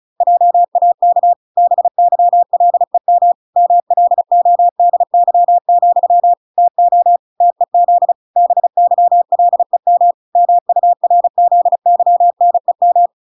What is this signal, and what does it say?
Morse code, heard as a keyed tone, about 34 wpm, 700 Hz. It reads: JAK BYLEM MLODY, TO TEZ BYLEM MURZYNEM